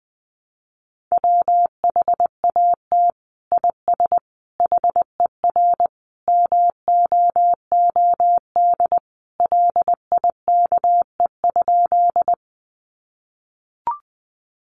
Morse code, heard as a keyed tone, about 20 wpm, 700 Hz. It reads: WHAT IS HER MOOD LIKE?
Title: Sentences from Top 1100 Words 20wpm